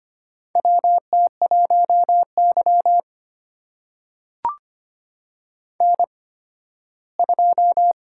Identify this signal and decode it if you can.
Morse code, heard as a keyed tone, about 25 wpm, spaced as for 15 wpm, 700 Hz. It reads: WT1Y N2
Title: USA Call Signs - 4 Characters - 8x Character Spacing - 25wpm